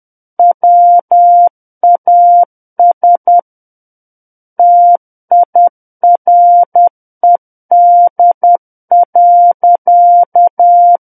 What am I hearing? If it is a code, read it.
Morse code, heard as a keyed tone, about 10 wpm, 700 Hz. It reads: WAS TIRED.
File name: Book One / 01-Chapter 1 - 10 WPM